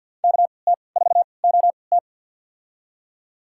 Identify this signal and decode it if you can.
Morse code, heard as a keyed tone, about 52 wpm, spaced as for 29 wpm, 700 Hz. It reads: KT4OT
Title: USA Call Signs - 5 Characters - 3x Character Spacing - 50wpm